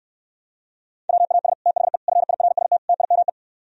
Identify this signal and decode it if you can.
Morse code, heard as a keyed tone, about 74 wpm, 700 Hz. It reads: WAS THE FIRST TIME